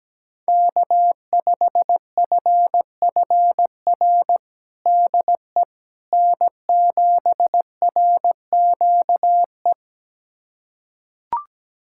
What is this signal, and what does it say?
Morse code, heard as a keyed tone, about 17 wpm, 700 Hz. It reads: K5FFR DE N7RQE
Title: USA Callsign DE Callsign - 5 Characters - 17wpm